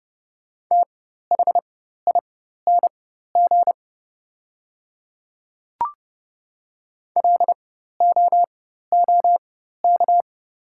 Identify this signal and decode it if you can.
Morse code, heard as a keyed tone, about 30 wpm, spaced as for 14 wpm, 700 Hz. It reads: THING LOOK
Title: 4x Character Spacing - Top 200 Words - 30wpm